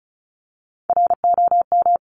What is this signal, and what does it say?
Morse code, heard as a keyed tone, about 34 wpm, 700 Hz. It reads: ROM
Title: Top Three Letter Combinations in Words 35wpm